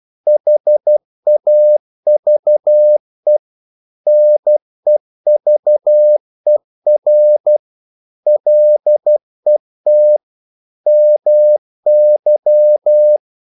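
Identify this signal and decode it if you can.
Morse code, heard as a keyed tone, about 12 wpm, 600 Hz. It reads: HAVE NEVER LET MY
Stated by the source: Morse practice sending English quotations.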